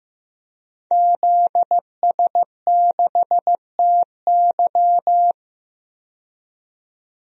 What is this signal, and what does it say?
Morse code, heard as a keyed tone, about 15 wpm, 700 Hz. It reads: ZS6TY
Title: DXCC-100 Call Signs - 5 Characters - 15wpm